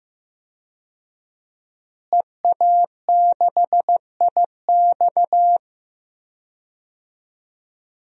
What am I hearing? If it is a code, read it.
Morse code, heard as a keyed tone, about 15 wpm, 700 Hz. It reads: EA6IX
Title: DXCC-100 Call Signs - 5 Characters - Repeated 3x - 15wpm